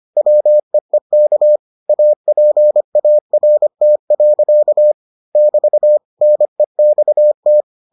Morse code, heard as a keyed tone, about 25 wpm, 600 Hz. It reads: WEEK APART. <BT> NEXT